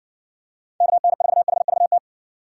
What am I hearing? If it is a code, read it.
Morse code, heard as a keyed tone, about 59 wpm, 700 Hz. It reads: KN4HVT